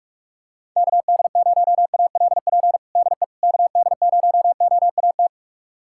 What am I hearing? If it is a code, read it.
Morse code, heard as a keyed tone, about 42 wpm, 700 Hz. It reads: KD0ALP DE KD0OAT
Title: USA Callsign DE Callsign 45wpm